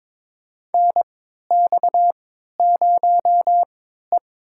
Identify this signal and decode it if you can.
Morse code, heard as a keyed tone, about 23 wpm, spaced as for 13 wpm, 700 Hz. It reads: NX0E
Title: USA Call Signs - 4 Characters - 3x Character Spacing - 22wpm